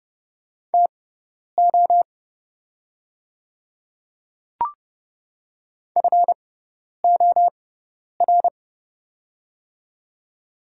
Morse code, heard as a keyed tone, about 30 wpm, spaced as for 10 wpm, 690 Hz. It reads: TO FOR
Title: 6x Character Spacing - Top 200 Words - 30wpm